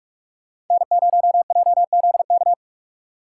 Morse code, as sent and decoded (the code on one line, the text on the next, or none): -. ----- .--- --.. -.-
N0JZK